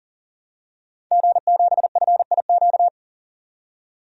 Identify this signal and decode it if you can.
Morse code, heard as a keyed tone, about 40 wpm, 700 Hz. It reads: G7FIQ
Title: DXCC-100 Call Signs - 5 Characters - 40wpm